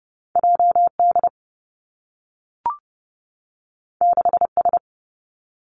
Morse code, heard as a keyed tone, about 30 wpm, 700 Hz. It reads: J D 6 S
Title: Two Challenging Letter-Number Combinations 30wpm